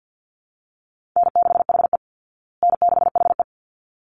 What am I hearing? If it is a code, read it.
Morse code, heard as a keyed tone, about 48 wpm, 700 Hz. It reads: N6HE N6HE
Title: Easy Call Signs Repeated 2x - 50wpm